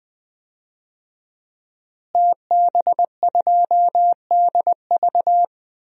Morse code, heard as a keyed tone, about 20 wpm, 700 Hz. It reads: TB2DV
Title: DXCC-100 Call Signs - 5 Characters - Repeated 2x - 20wpm